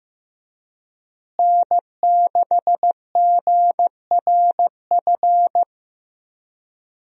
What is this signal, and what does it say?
Morse code, heard as a keyed tone, about 15 wpm, 700 Hz. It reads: N6GRF
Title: USA Call Signs - 5 Characters - Repeated 3x - 15wpm